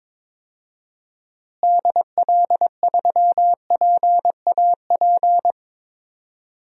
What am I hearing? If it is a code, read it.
Morse code, heard as a keyed tone, about 21 wpm, 700 Hz. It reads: DL3PAP
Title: DXCC-100 Call Signs - 6 Characters - Repeated 2x - 22wpm